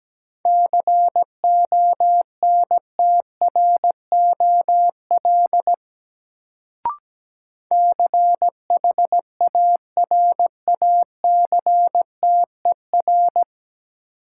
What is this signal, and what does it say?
Morse code, heard as a keyed tone, about 17 wpm, 700 Hz. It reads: CONTROL CHARACTER
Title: Sentences from Top 1200 Words - New Word Drill Alt - 17wpm